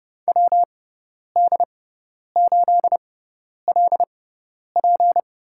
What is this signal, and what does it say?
Morse code, heard as a keyed tone, about 30 wpm, spaced as for 10 wpm, 700 Hz. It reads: WD8LP
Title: USA Call Signs - 6x Character Spacing - 30wpm